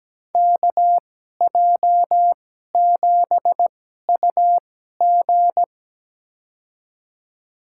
Morse code, heard as a keyed tone, about 17 wpm, spaced as for 12 wpm, 700 Hz. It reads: KJ7UG
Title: USA Call Signs - 2x Character Spacing - 17wpm